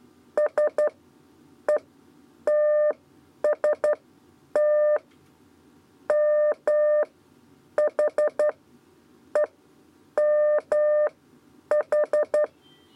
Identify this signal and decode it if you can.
Morse code, heard as a keyed tone, about 11 wpm, spaced as for 8 wpm, 600 Hz. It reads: SETST MHEMH
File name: Lezione trasmessa 2 lez 2 exe